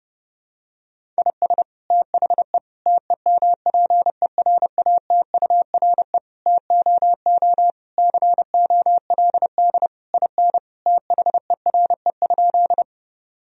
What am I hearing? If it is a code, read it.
Morse code, heard as a keyed tone, about 30 wpm, 700 Hz. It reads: IS THE TEMPERATURE TOO COLD IN THERE?
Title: Sentences from Top 1000 Words - Review Part 4 - 30wpm